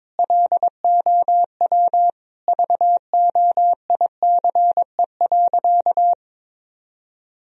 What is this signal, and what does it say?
Morse code, heard as a keyed tone, about 22 wpm, 700 Hz. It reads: LOW VOICE.
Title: Sentences from Top 1000 Words - Review Part 1 - 22wpm